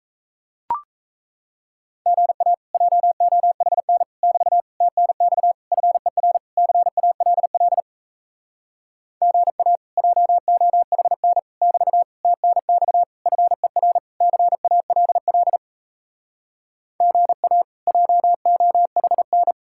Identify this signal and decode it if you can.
Morse code, heard as a keyed tone, about 37 wpm, 700 Hz. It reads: GA JOHN <BT> TNX FER CALL GA JOHN <BT> TNX FER CALL GA JOHN